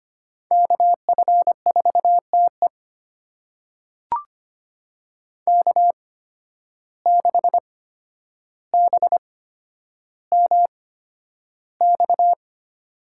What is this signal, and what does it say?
Morse code, heard as a keyed tone, about 25 wpm, spaced as for 11 wpm, 700 Hz. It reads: KF4TE K6BMX
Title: USA Call Signs - 5 Characters - 8x Character Spacing - 25wpm